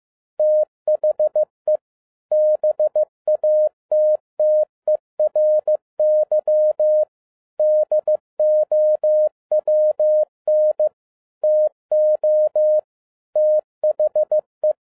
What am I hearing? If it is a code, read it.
Morse code, heard as a keyed tone, about 15 wpm, 610 Hz. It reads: THE BATTERY DOWN TO THE